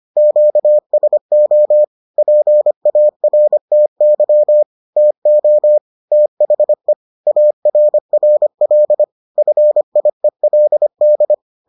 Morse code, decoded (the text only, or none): QSO PARTY TO THE ARRL FIELD